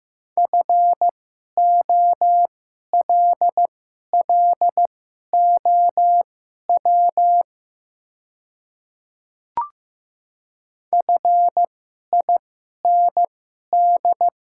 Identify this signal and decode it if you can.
Morse code, heard as a keyed tone, about 15 wpm, spaced as for 11 wpm, 700 Hz. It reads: FOLLOW FIND
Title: 2x Character Spacing - Top 200 Words - 15wpm